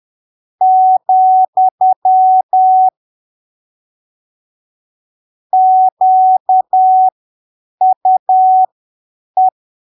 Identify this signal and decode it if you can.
Morse code, heard as a keyed tone, about 10 wpm, spaced as for 7 wpm, 750 Hz. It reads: , QUE
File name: arquivos / texto / texto 07 10ppm